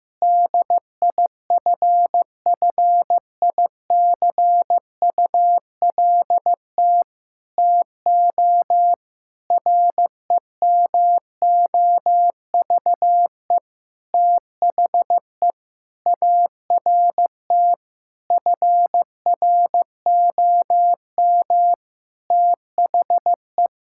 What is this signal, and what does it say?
Morse code, heard as a keyed tone, about 15 wpm, 700 Hz. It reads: DIFFICULT TO REMOVE THE ART FROM THE
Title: Sentences from Top 1000 Words - Review Part 6 - 15wpm